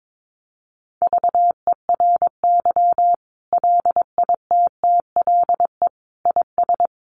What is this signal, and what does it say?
Morse code, heard as a keyed tone, about 22 wpm, 700 Hz. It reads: VERY LITTLE IS
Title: Sentences from Top 1500 Words 22wpm